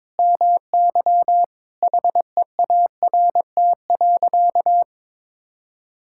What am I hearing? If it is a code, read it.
Morse code, heard as a keyed tone, about 22 wpm, 700 Hz. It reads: MY HEART.
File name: practice / 22wpm / Sentences with 8-Word Phrases - 22wpm